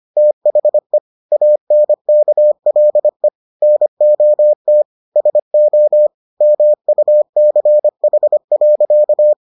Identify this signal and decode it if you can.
Morse code, heard as a keyed tone, about 25 wpm, 600 Hz. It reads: THE ANKLE NOT SO MUCH.